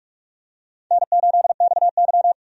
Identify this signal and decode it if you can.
Morse code, heard as a keyed tone, about 42 wpm, 700 Hz. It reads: N8XY
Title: USA Call Signs - 4 Characters - 45wpm